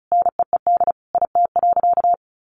Morse code, heard as a keyed tone, about 34 wpm, 700 Hz. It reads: NEED IT.
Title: Sentences from Top 1400 Words 35wpm